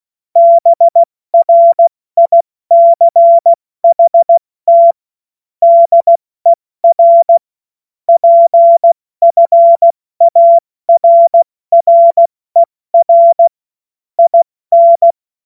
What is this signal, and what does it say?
Morse code, heard as a keyed tone, about 16 wpm, 680 Hz. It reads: BRICHT DER PFARRER IN
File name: Morsekurs / Witze / Witz 06. September 2022 70 F80